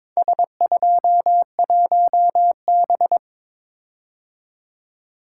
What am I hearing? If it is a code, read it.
Morse code, heard as a keyed tone, about 22 wpm, 700 Hz. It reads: S21B
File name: practice / 22wpm / Four Letter-Number Combinations - 22wpm